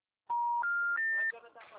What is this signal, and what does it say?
Special information tone, with speech.
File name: out- -2016- - - .92